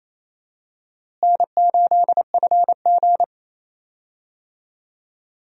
Morse code, heard as a keyed tone, about 28 wpm, 700 Hz. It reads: N8FG
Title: USA Call Signs - 4 Characters - Repeated 2x - 28wpm